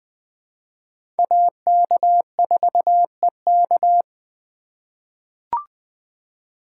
Morse code, heard as a keyed tone, about 20 wpm, 700 Hz. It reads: AK4EK
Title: USA Call Signs - 3x Character Spacing - 20wpm